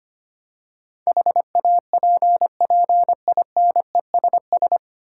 Morse code, heard as a keyed tone, about 25 wpm, 700 Hz. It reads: HAPPINESS
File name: Nine Letter Words - 25wpm